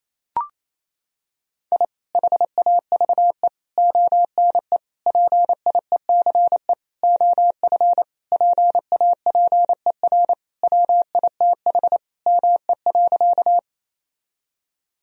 Morse code, heard as a keyed tone, about 27 wpm, 700 Hz. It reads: I HAVE ONE PIECE OF PAPER WITH ME.
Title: Sentences from Top 1000 Words - Review Part 2 - 28wpm